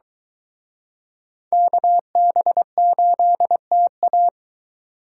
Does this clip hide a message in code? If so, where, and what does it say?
Yes: Morse code from 1.5 to 4.3 s, heard as a keyed tone, about 23 wpm, 700 Hz. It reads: KB8TA